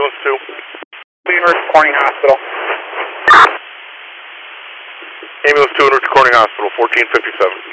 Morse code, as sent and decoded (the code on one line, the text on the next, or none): none